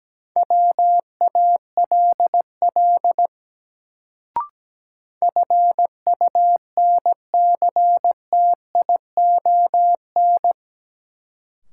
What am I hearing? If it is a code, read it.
Morse code, heard as a keyed tone, about 17 wpm, 700 Hz. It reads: WALL FUNCTION
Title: Sentences from Top 1100 Words - New Word Drill Alt - 17wpm